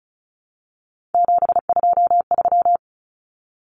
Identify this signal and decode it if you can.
Morse code, heard as a keyed tone, about 34 wpm, 700 Hz. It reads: 723
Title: Common Cut Numbers - 3-Digit 35wpm